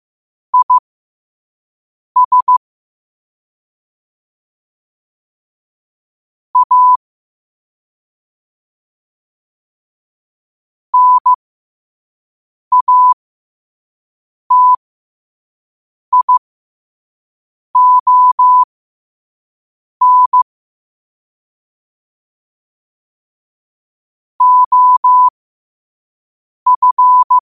Morse code, heard as a keyed tone, about 15 wpm, spaced as for 5 wpm, 1000 Hz. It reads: IS A NATION OF
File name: qotdfiles / 5wpm